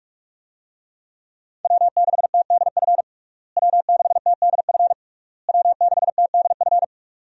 Morse code, heard as a keyed tone, about 46 wpm, 700 Hz. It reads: W6TDF W6TDF W6TDF
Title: USA Call Signs - 5 Characters - Repeated 3x - 45wpm